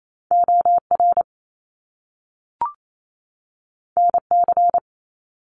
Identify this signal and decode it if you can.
Morse code, heard as a keyed tone, about 28 wpm, 700 Hz. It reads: O R N C
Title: Top Two Letter Combinations in Words 28wpm